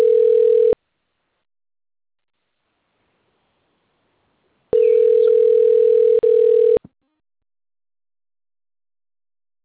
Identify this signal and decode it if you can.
Ringback tone (North America), 440 + 480 Hz.
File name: rg-1600-unknown- - - .2393